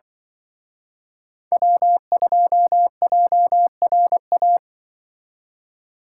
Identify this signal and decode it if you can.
Morse code, heard as a keyed tone, about 25 wpm, 700 Hz. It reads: W2JRA
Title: Speed Racing - USA 1x3 Call Signs 20wpm